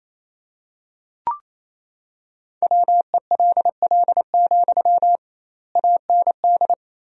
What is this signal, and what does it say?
Morse code, heard as a keyed tone, about 28 wpm, 700 Hz. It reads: WELL, AND
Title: 2x Word-Spacing - Sentences from Top 200 Words - 28wpm